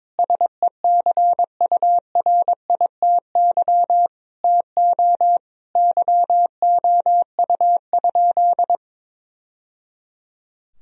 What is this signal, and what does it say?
Morse code, heard as a keyed tone, about 22 wpm, 700 Hz. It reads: SECURITY TO YOU?
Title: Sentences from Top 1000 Words - Review Part 4 - 22wpm